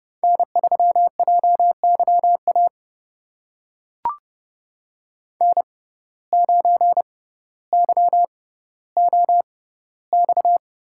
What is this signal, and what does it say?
Morse code, heard as a keyed tone, about 30 wpm, spaced as for 15 wpm, 700 Hz. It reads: N3JYA N9YOX
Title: USA Call Signs - 5 Characters - 6x Character Spacing - 30wpm